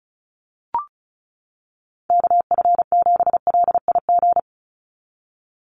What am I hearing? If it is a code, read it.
Morse code, heard as a keyed tone, about 34 wpm, 700 Hz. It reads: KF7LIG
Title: USA Call Signs 35wpm